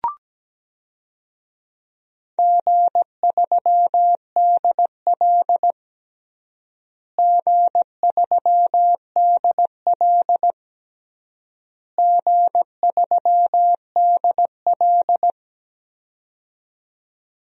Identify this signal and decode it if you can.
Morse code, heard as a keyed tone, about 17 wpm, 700 Hz. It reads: G3DL G3DL G3DL
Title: DXCC-100 Call Signs - 4 Characters - Repeated 3x - 17wpm